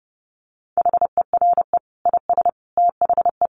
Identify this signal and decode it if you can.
Morse code, heard as a keyed tone, about 30 wpm, 700 Hz. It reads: HERE IS THE
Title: Sentences from Top 300 Words 30wpm